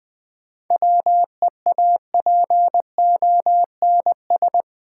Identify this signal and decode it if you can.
Morse code, heard as a keyed tone, about 20 wpm, 690 Hz. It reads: WEAPONS
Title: Build A Phrase - 4-Word Phrases - 20wpm